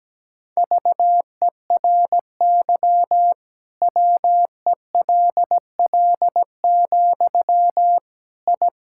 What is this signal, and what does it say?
Morse code, heard as a keyed tone, about 17 wpm, 700 Hz. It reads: VERY WELL, I
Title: Sentences from Top 1000 Words - Review Part 5 - 17wpm